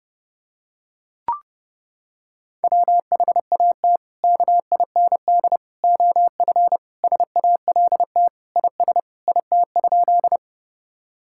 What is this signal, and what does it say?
Morse code, heard as a keyed tone, about 30 wpm, 700 Hz. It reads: WHAT KIND OF SALT IS IT?
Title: Sentences from Top 1000 Words - Review Part 5 - 30wpm